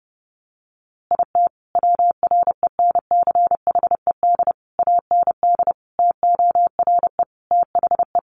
Morse code, heard as a keyed tone, about 30 wpm, 700 Hz. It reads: IT WRENCHED AND TORE THE
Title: A Sound Of Thunder 30wpm